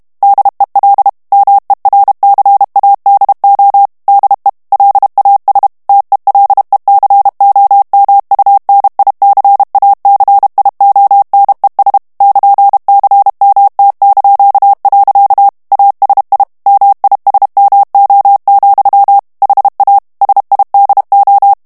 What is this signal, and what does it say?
Morse code, heard as a keyed tone, about 32 wpm, 800 Hz. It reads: DEL MERCADO DE LAS TELECOMUNICACIONES <KN>CMT). ASI MISMO, HA SIDO
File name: ESPAGNOL TEXTE EN CLAIR 2 VITESSE 30 MOTS MINUTES